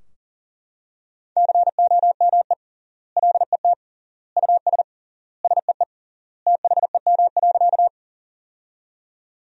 Morse code, heard as a keyed tone, about 40 wpm, 700 Hz. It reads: COME LET US SEE THEM.
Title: 3x Word-Spacing - Sentences from Top 300 Words - 40wpm